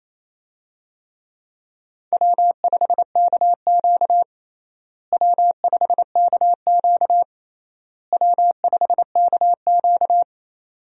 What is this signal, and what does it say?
Morse code, heard as a keyed tone, about 28 wpm, 700 Hz. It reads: W5KQ W5KQ W5KQ
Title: USA Call Signs - 4 Characters - Repeated 3x - 28wpm